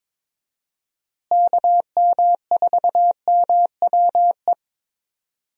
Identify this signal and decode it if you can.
Morse code, heard as a keyed tone, about 22 wpm, 700 Hz. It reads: KM4MWE
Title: USA Call Signs - 6 Characters - Repeated 3x - 22wpm